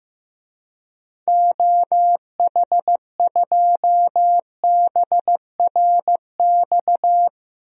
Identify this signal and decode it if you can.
Morse code, heard as a keyed tone, about 15 wpm, 700 Hz. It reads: OH2BRX
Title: DXCC-100 Call Signs - 6 Characters - 15wpm